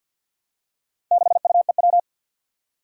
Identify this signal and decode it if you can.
Morse code, heard as a keyed tone, about 47 wpm, 700 Hz. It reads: 6UEW
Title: Four Challenging Letter-Number Combinations 50wpm